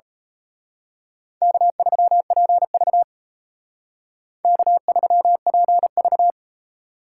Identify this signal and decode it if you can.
Morse code, heard as a keyed tone, about 32 wpm, 700 Hz. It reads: K3PV K3PV
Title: Speed Racing - USA 1x2 Call Signs 28wpm